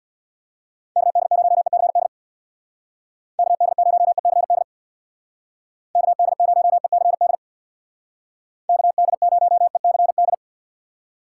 Morse code, heard as a keyed tone, about 64 wpm, 700 Hz. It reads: KD0ECD KD0ECD KD0ECD KD0ECD